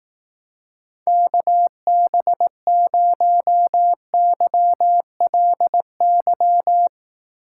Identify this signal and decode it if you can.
Morse code, heard as a keyed tone, about 18 wpm, 700 Hz. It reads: KB0YLY
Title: Speed Racing - USA 2x3 Call Signs 15wpm